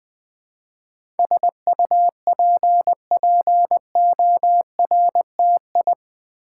Morse code, heard as a keyed tone, about 20 wpm, 700 Hz. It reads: SUPPORTI
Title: Build A Word - 4 Letters At A Time - 10-Letter Words - 20wpm